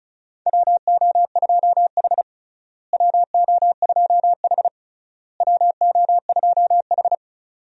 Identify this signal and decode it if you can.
Morse code, heard as a keyed tone, about 36 wpm, 700 Hz. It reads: WO2H WO2H WO2H
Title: USA Call Signs - 4 Characters - Repeated 3x - 35wpm